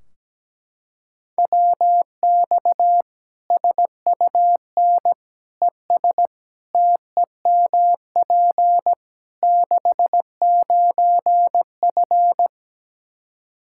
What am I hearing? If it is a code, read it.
Morse code, heard as a keyed tone, about 17 wpm, 700 Hz. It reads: WX SUN ES TEMP 69F
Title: K1USN SST QSO Practice 17wpm